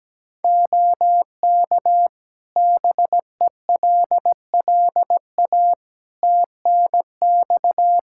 Morse code, heard as a keyed tone, about 17 wpm, 700 Hz. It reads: OK BELLA TNX
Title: K1USN SST QSO Practice 17wpm